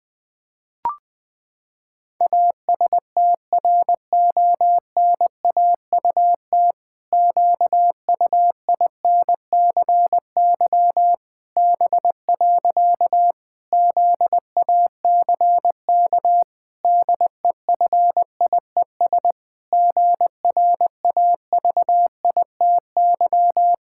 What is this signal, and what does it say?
Morse code, heard as a keyed tone, about 20 wpm, 700 Hz. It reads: ASTRONAUT QUINCY B. ZACK DEFIES GRAVITY
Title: Pangram Challenge v2 - 20wpm